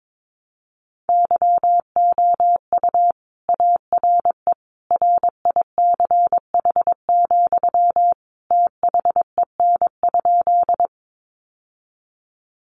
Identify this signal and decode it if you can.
Morse code, heard as a keyed tone, about 22 wpm, 700 Hz. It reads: YOU ARE RICH, THEN?
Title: Sentences from Top 1100 Words 22wpm